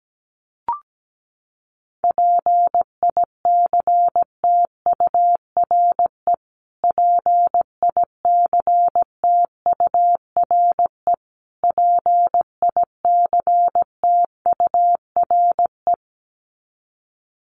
Morse code, heard as a keyed tone, about 17 wpm, 700 Hz. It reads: PICTURE PICTURE PICTURE
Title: Top 500 Words Repeated 17wpm